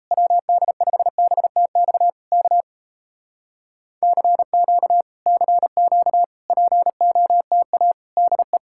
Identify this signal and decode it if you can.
Morse code, heard as a keyed tone, about 37 wpm, 700 Hz. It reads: WD5BTX K CQ CQ POTA DE